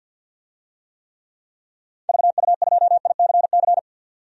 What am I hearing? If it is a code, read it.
Morse code, heard as a keyed tone, about 48 wpm, 700 Hz. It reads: VU2ICC
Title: DXCC-100 Call Signs - 6 Characters - 50wpm